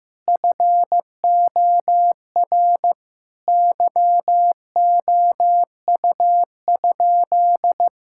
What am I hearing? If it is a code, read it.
Morse code, heard as a keyed tone, about 15 wpm, 700 Hz. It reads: FOR YOU?
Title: Sentences from Top 1600 Words 15wpm